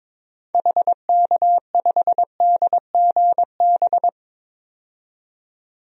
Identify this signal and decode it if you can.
Morse code, heard as a keyed tone, about 22 wpm, 700 Hz. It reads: HK5DGB
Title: DXCC-100 Call Signs - 6 Characters - 22wpm